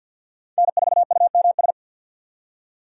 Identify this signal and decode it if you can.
Morse code, heard as a keyed tone, about 53 wpm, 700 Hz. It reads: N4UMS
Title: USA Call Signs - 5 Characters - 2x Character Spacing - 50wpm